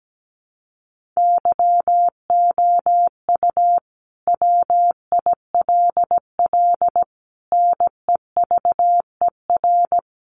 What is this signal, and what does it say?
Morse code, heard as a keyed tone, about 17 wpm, 700 Hz. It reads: YOU WILL NEVER
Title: Sentences from Top 1200 Words 17wpm